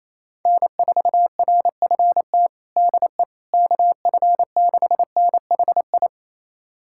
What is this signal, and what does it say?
Morse code, heard as a keyed tone, about 28 wpm, 700 Hz. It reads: N4RFT DE KF6NHI
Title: USA Callsign DE Callsign 28wpm